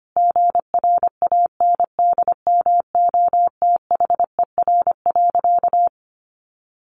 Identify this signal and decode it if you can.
Morse code, heard as a keyed tone, about 25 wpm, 700 Hz. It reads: GRANDMOTHER.
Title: Sentences from Top 1200 Words 25wpm